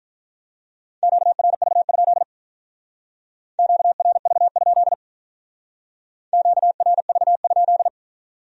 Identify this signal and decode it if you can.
Morse code, heard as a keyed tone, about 43 wpm, 700 Hz. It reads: QRV? QRV? QRV?
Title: Speed-Racing Q-Codes - 35wpm